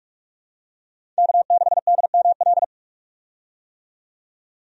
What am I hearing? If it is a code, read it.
Morse code, heard as a keyed tone, about 47 wpm, 700 Hz. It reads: K6DML
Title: USA Call Signs - 5 Characters - 45wpm